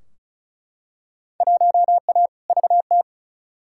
Morse code, heard as a keyed tone, about 34 wpm, 700 Hz. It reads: 1A VT
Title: Field Day Exchanges Encore - 35wpm